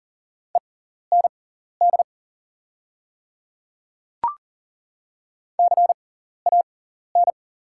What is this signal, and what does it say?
Morse code, heard as a keyed tone, about 40 wpm, spaced as for 14 wpm, 700 Hz. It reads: END CAN
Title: 6x Character Spacing - Top 300 Words - 40wpm